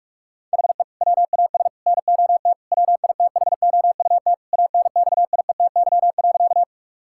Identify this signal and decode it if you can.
Morse code, heard as a keyed tone, about 41 wpm, 700 Hz. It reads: HE WAS NOT WITHOUT ANXIETY.
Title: Sentences from Top 1600 Words 45wpm